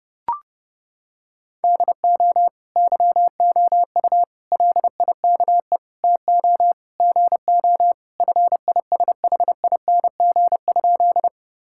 Morse code, heard as a keyed tone, about 32 wpm, 700 Hz. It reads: DO YOU LIKE TO GO FISHING?
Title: Sentences from Top 1000 Words - Review Part 5 - 30wpm